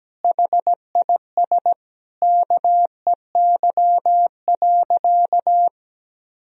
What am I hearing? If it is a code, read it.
Morse code, heard as a keyed tone, about 17 wpm, 700 Hz. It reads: HIS KEY.
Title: Sentences from Top 1000 Words - Review Part 4 - 17wpm